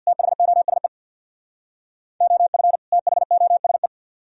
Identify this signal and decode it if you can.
Morse code, heard as a keyed tone, about 63 wpm, 700 Hz. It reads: THOSE OF THOSE